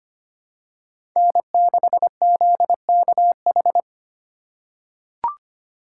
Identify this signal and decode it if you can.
Morse code, heard as a keyed tone, about 25 wpm, 700 Hz. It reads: N6ZKH
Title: USA Call Signs - 5 Characters - 2x Character Spacing - 25wpm